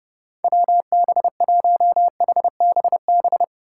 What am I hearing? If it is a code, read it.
Morse code, heard as a keyed tone, about 30 wpm, 700 Hz. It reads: WB1HBB